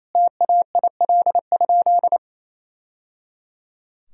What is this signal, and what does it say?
Morse code, heard as a keyed tone, about 27 wpm, 700 Hz. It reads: TAIL?
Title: Sentences from Top 1000 Words - Review Part 3 - 28wpm